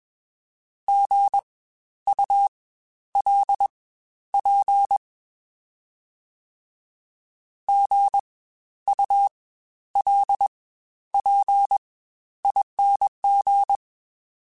Morse code, heard as a keyed tone, about 21 wpm, spaced as for 11 wpm, 800 Hz. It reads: GULP GULPING
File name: Spaced Alphas